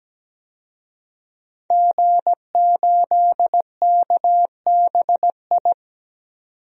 Morse code, heard as a keyed tone, about 17 wpm, 700 Hz. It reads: G8KBI
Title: DXCC-100 Call Signs - 5 Characters - 17wpm